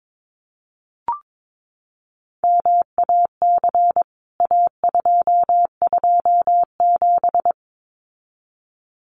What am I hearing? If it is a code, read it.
Morse code, heard as a keyed tone, about 22 wpm, 700 Hz. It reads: MAC A227
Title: CWT Exchange 22wpm